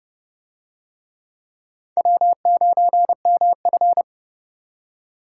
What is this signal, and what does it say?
Morse code, heard as a keyed tone, about 29 wpm, 700 Hz. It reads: W9MF